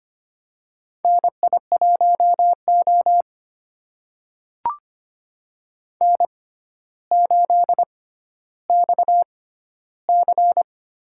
Morse code, heard as a keyed tone, about 25 wpm, spaced as for 13 wpm, 700 Hz. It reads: NI1O N8XC
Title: USA Call Signs - 4 Characters - 6x Character Spacing - 25wpm